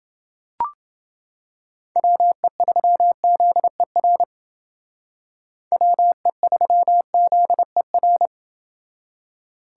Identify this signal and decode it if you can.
Morse code, heard as a keyed tone, about 29 wpm, 700 Hz. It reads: WE3ZER WE3ZER